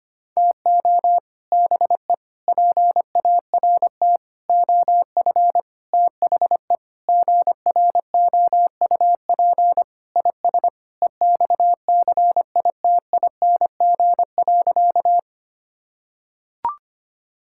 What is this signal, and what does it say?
Morse code, heard as a keyed tone, about 25 wpm, 700 Hz. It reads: TO BE PART OF THE GROUP IS EXCITING.
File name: practice / 25wpm / Sentences with 4-Word Phrases - 25wpm